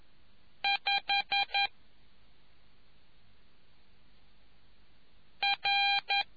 Morse code, heard as a keyed tone, about 11 wpm, 2400 Hz. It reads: 5 R